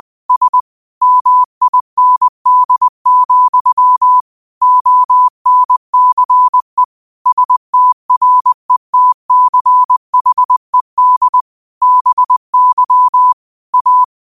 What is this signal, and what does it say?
Morse code, heard as a keyed tone, about 20 wpm, 1000 Hz. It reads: S MIND, ONCE STRETCHED BY A
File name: qotdfiles / 20wpm